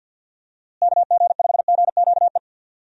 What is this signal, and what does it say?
Morse code, heard as a keyed tone, about 50 wpm, 700 Hz. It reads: KG5GQE
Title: USA Call Signs - 6 Characters - Repeated 3x - 50wpm